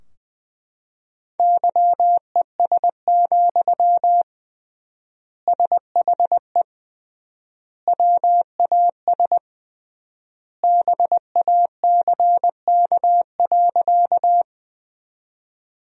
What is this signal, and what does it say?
Morse code, heard as a keyed tone, about 20 wpm, 700 Hz. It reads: YES, SHE WAS BACK.
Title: 3x Word-Spacing - Sentences from Top 500 Words - 20wpm